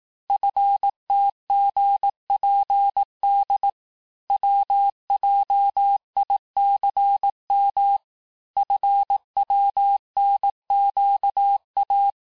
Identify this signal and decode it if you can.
Morse code, heard as a keyed tone, about 18 wpm, 800 Hz. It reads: FTGPD WJICM FWNQA
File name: cours-du-lundi files / cours-1588-texte-chiffre-vitesse-17MPM